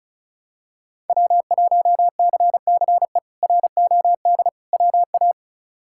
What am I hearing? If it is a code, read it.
Morse code, heard as a keyed tone, about 36 wpm, 700 Hz. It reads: W1CCE ROD WA
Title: K1USN SST QSO Practice - 35wpm